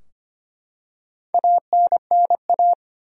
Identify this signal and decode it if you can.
Morse code, heard as a keyed tone, about 25 wpm, 700 Hz. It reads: ANNA
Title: USA Common Names - 25wpm